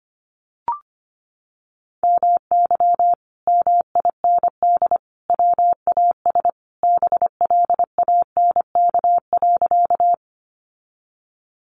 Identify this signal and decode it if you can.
Morse code, heard as a keyed tone, about 25 wpm, 700 Hz. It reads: MY MIND WAS BLANK.
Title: Sentences from Top 1500 Words 25wpm